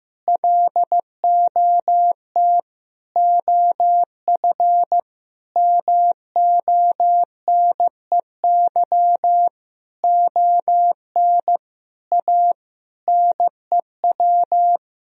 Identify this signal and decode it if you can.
Morse code, heard as a keyed tone, about 15 wpm, 700 Hz. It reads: LOT OF MONEY ON A NEW
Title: Sentences from Top 1000 Words - Review Part 4 - 15wpm